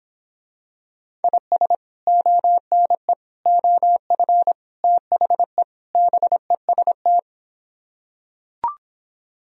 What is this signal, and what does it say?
Morse code, heard as a keyed tone, about 26 wpm, 700 Hz. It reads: IS ONE OF THE BEST